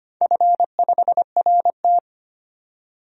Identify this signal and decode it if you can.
Morse code, heard as a keyed tone, about 25 wpm, 700 Hz. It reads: F5RT
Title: DXCC-100 Call Signs - 4 Characters - 25wpm